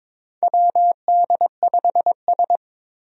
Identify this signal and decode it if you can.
Morse code, heard as a keyed tone, about 22 wpm, 700 Hz. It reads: WD5S
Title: USA Call Signs - 4 Characters - Repeated 2x - 22wpm